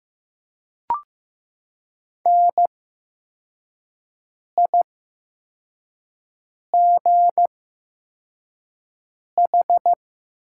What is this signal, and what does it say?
Morse code, heard as a keyed tone, about 15 wpm, spaced as for 4 wpm, 700 Hz. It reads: NIGH
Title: 8x Character Spacing - Top 300 Words - 15wpm